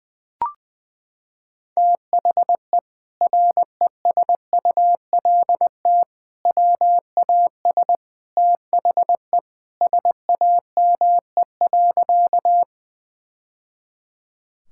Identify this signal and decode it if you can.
Morse code, heard as a keyed tone, about 20 wpm, 700 Hz. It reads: THE RESULT WAS THE SAME.
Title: Sentences from Top 1000 Words - Review Part 5 - 20wpm